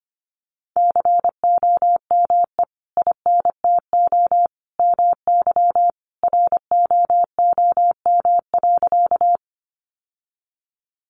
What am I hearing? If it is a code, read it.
Morse code, heard as a keyed tone, about 25 wpm, 700 Hz. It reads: COME INTO MY ROOM.
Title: Sentences from Top 600 Words 25wpm